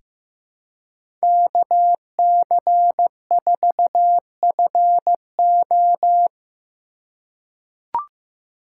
Morse code, heard as a keyed tone, about 15 wpm, 700 Hz. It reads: KC4FO
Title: USA Call Signs - 5 Characters - 2x Character Spacing - 15wpm